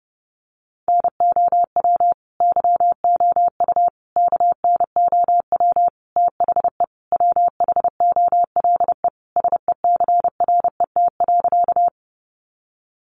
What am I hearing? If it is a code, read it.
Morse code, heard as a keyed tone, about 30 wpm, 700 Hz. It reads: NOW YOU KNOW THE WHOLE SECRET.
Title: Sentences from Top 1200 Words 30wpm